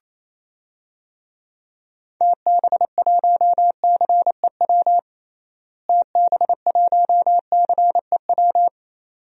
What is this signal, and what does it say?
Morse code, heard as a keyed tone, about 28 wpm, 700 Hz. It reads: TB1CEW TB1CEW
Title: DXCC-100 Call Signs - 6 Characters - Repeated 2x - 28wpm